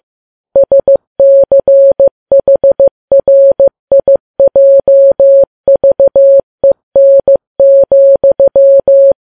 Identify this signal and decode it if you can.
Morse code, heard as a keyed tone, about 15 wpm, 550 Hz. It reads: SCHRIJVEN,
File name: Files Web tls / Morse on9fv / morse15kps